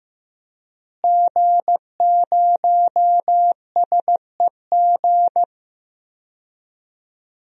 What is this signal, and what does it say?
Morse code, heard as a keyed tone, about 16 wpm, 700 Hz. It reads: G0SEG
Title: DXCC-100 Call Signs - 5 Characters - Repeated 3x - 15wpm